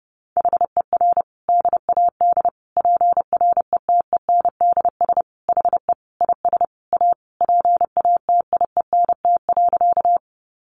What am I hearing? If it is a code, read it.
Morse code, heard as a keyed tone, about 28 wpm, 700 Hz. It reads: HER DAD PRETENDS HE IS A PATIENT.
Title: Additional Easy Phrases 30wpm